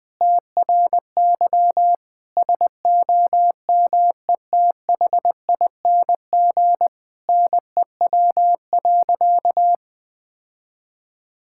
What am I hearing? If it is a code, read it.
Morse code, heard as a keyed tone, about 19 wpm, 700 Hz. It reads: TRY SOMETHING NEW.
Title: Sentences with 8-Word Phrases - Spoken Beginning - 20wpm